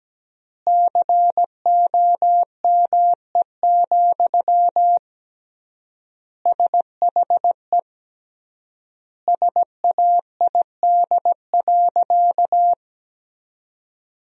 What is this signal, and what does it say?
Morse code, heard as a keyed tone, about 17 wpm, 700 Hz. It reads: COME, SHE SAID.
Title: 3x Word-Spacing - Sentences from Top 100 Words - 17wpm